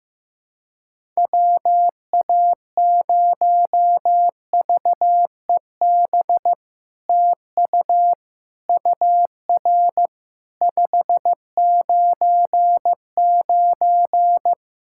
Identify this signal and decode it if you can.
Morse code, heard as a keyed tone, about 15 wpm, 700 Hz. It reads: WA0VEB TU UR 599
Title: SOTA QSO Practice 15wpm